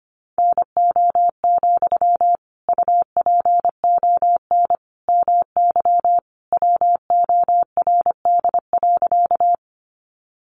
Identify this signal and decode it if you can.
Morse code, heard as a keyed tone, about 25 wpm, 700 Hz. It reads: NO, UPON MY WORD.
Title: Sentences from Top 1100 Words 25wpm